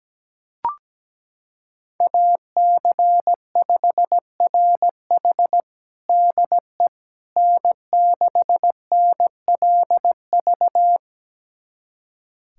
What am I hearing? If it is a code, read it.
Morse code, heard as a keyed tone, about 17 wpm, 700 Hz. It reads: AC5RH DE N6NLV
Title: USA Callsign DE Callsign 17wpm